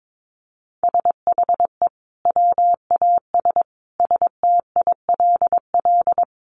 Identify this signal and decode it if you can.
Morse code, heard as a keyed tone, about 22 wpm, 700 Hz. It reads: SHE WAS STILL
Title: Sentences from Top 500 Words 22wpm